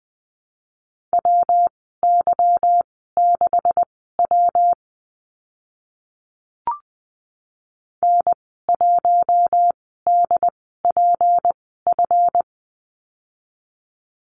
Morse code, heard as a keyed tone, about 20 wpm, spaced as for 15 wpm, 700 Hz. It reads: WY6W N1DPF
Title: 154 Lesson 25g - Letter p - Call Signs 20wpm